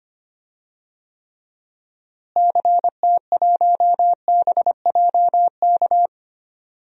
Morse code, heard as a keyed tone, about 25 wpm, 700 Hz. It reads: CT1BJK
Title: DXCC-100 Call Signs - 6 Characters - Repeated 2x - 25wpm